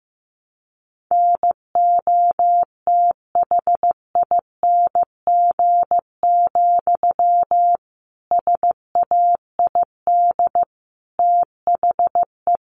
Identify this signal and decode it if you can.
Morse code, heard as a keyed tone, about 15 wpm, 700 Hz. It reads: NOTHING, SAID THE
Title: Sentences from Top 1300 Words 15wpm